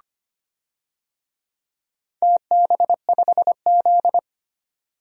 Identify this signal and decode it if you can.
Morse code, heard as a keyed tone, about 25 wpm, 700 Hz. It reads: TB5Z
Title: DXCC-100 Call Signs - 4 Characters - 25wpm